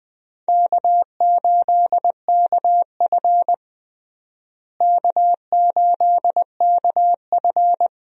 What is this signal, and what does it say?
Morse code, heard as a keyed tone, about 21 wpm, 700 Hz. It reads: K8KF K8KF
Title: USA Call Signs - 4 Characters - Repeated 2x - 20wpm